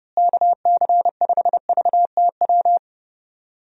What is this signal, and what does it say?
Morse code, heard as a keyed tone, about 30 wpm, 700 Hz. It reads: KC5VTW
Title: USA Call Signs - 6 Characters - Repeated 2x - 30wpm